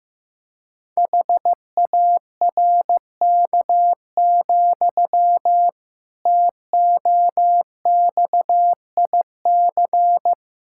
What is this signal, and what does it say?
Morse code, heard as a keyed tone, about 15 wpm, 700 Hz. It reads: HARK, TOXIC
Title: Pangram Challenge v2 - 15wpm